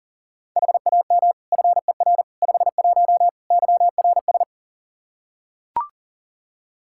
Morse code, heard as a keyed tone, about 37 wpm, 700 Hz. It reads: HAM FER 51 YRS
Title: Practice QSO - Exchange 3 - More Info - 40wpm